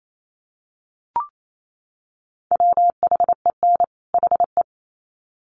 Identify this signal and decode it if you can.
Morse code, heard as a keyed tone, about 28 wpm, 700 Hz. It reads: WHEN HE
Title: Sets of 2 Words 28wpm